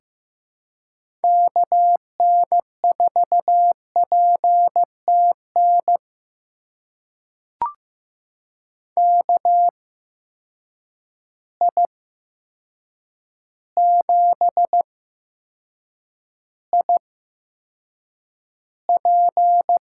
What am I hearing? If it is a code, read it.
Morse code, heard as a keyed tone, about 15 wpm, spaced as for 7 wpm, 700 Hz. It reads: KN4PTN KI7IP
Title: USA Call Signs - 6 Characters - 8x Character Spacing - 15wpm